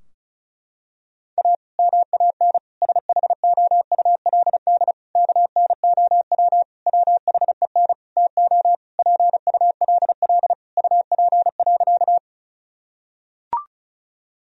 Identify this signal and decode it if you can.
Morse code, heard as a keyed tone, about 36 wpm, 700 Hz. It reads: A MAN SHOULD KNOW WHEN TO PULL UP.
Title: Sentences from Top 1000 Words - Review Part 2 - 35wpm